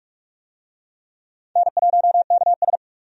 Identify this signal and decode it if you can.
Morse code, heard as a keyed tone, about 45 wpm, 700 Hz. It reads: N1KS
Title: USA Call Signs - 4 Characters - 45wpm